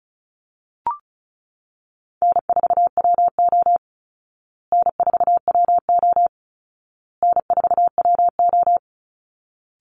Morse code, heard as a keyed tone, about 34 wpm, 700 Hz. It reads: N4WO N4WO N4WO
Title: Call Signs Repeated 2x - 35wpm